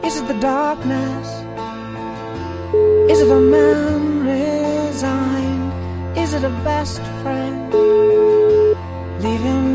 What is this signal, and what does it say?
Ringback tone (Europe, 425 Hz), with noise.